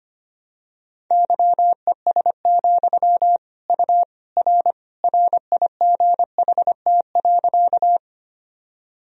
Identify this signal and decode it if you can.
Morse code, heard as a keyed tone, about 25 wpm, 700 Hz. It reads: YES, U R RIGHT.
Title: Sentences from Top 500 Words and Abbreviations 25wpm